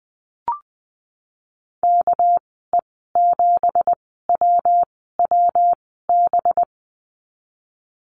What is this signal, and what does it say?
Morse code, heard as a keyed tone, about 20 wpm, spaced as for 14 wpm, 700 Hz. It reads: KE7WWB
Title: 200 Lesson 32g - Letter k - Call Signs 20wpm